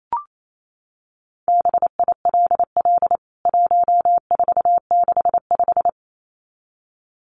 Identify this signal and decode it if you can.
Morse code, heard as a keyed tone, about 28 wpm, 700 Hz. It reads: BILL 1465
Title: CWT Exchange 28wpm